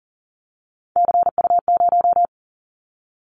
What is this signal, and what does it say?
Morse code, heard as a keyed tone, about 38 wpm, 700 Hz. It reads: CU0
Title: Callsign Prefixes for Top 50 DXCC - 40wpm